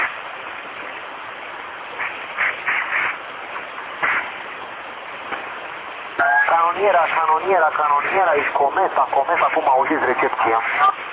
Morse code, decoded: none

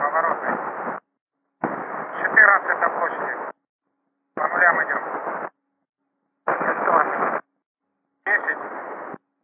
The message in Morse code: none

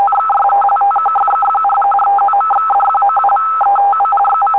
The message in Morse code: none

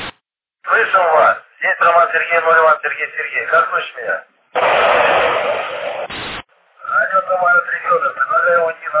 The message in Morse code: none

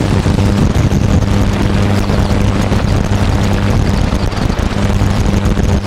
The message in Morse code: none

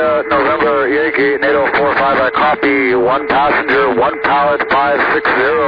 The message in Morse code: none

